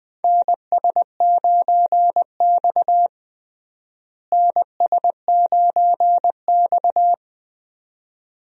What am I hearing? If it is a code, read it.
Morse code, heard as a keyed tone, about 20 wpm, 700 Hz. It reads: NS9X NS9X
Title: USA Call Signs - 4 Characters - Repeated 2x - 20wpm